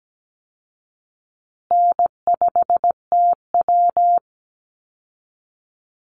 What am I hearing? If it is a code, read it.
Morse code, heard as a keyed tone, about 17 wpm, 700 Hz. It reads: N5TW
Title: Easy Call Signs Repeated 3x - 17wpm